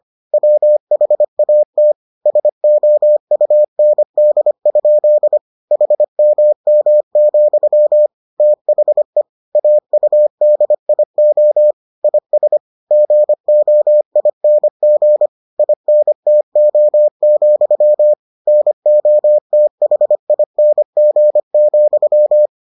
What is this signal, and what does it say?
Morse code, heard as a keyed tone, about 25 wpm, 600 Hz. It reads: WHAT SOUND? HMM, THE AUDIO IS GOING INTO, NOTHING,